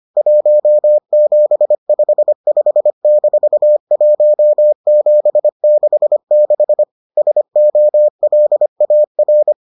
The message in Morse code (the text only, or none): 1755-1766 SOLAR